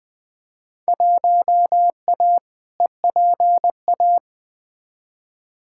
Morse code, heard as a keyed tone, about 20 wpm, 700 Hz. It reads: 1A EPA
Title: Field Day Exchanges - 20wpm